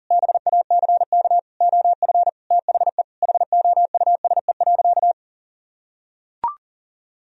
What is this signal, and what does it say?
Morse code, heard as a keyed tone, about 40 wpm, 700 Hz. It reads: BACK OF THE HOUSE.
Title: Sentences from Top 1000 Words - Review Part 1 - 40wpm